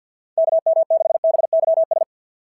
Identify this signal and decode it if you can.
Morse code, heard as a keyed tone, about 52 wpm, 690 Hz. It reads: KM6BYS
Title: SOTA QSO Practice 50wpm